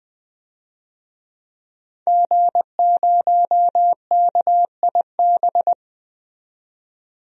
Morse code, heard as a keyed tone, about 20 wpm, 700 Hz. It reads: G0KIB
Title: DXCC-100 Call Signs - 5 Characters - 20wpm